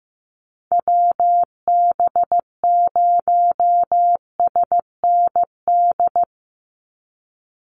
Morse code, heard as a keyed tone, about 15 wpm, 700 Hz. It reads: WB0SND
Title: Call Signs Repeated 3x - 15wpm